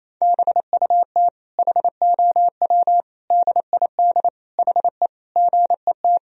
Morse code, heard as a keyed tone, about 27 wpm, 700 Hz. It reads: BUT HOW DID HE GET
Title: Sentences from Top 1000 Words - Review Part 4 - 28wpm